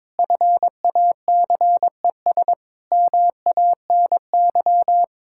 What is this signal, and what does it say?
Morse code, heard as a keyed tone, about 22 wpm, 700 Hz. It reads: FACES MANY
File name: Sentences with 7-Word Phrases - 22wpm